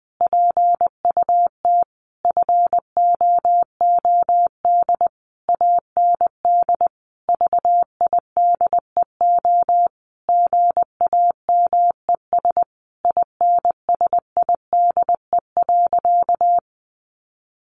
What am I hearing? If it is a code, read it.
Morse code, heard as a keyed tone, about 20 wpm, 700 Hz. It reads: PUT FOOD AND VIDEO GAMES INSIDE.
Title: Easy Phrases 20wpm